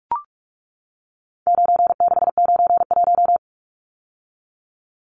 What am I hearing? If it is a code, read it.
Morse code, heard as a keyed tone, about 44 wpm, 700 Hz. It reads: 9691
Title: Numbers - 4-Digit 45wpm